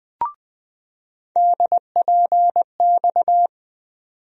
Morse code, heard as a keyed tone, about 20 wpm, 700 Hz. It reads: DPX
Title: USA Call Sign Postfixes - 3 Letters 20wpm